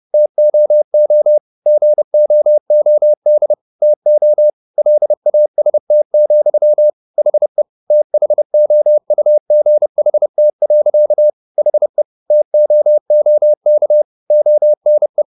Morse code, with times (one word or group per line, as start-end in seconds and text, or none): 0.1-1.4 TOO
1.7-3.5 GOOD
3.8-4.5 TO
4.8-6.9 LAST,
7.2-7.6 HE
7.9-11.3 THOUGHT.
11.6-12.0 HE
12.3-14.0 TOOK
14.3-15.2 ONE